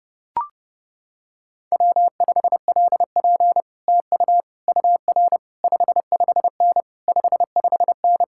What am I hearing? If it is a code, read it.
Morse code, heard as a keyed tone, about 30 wpm, 700 Hz. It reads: W5LP TU UR 55N 55N